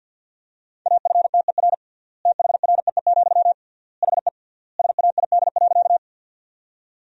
Morse code, heard as a keyed tone, about 50 wpm, 700 Hz. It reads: AFTER THREE, HE SAID.
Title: 3x Word-Spacing - Sentences from Top 200 Words - 50wpm